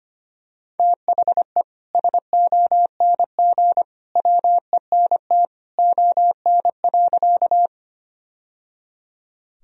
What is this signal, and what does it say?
Morse code, heard as a keyed tone, about 25 wpm, 700 Hz. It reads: THE SONG WENT ON.
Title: Sentences from Top 1000 Words - Review Part 2 - 25wpm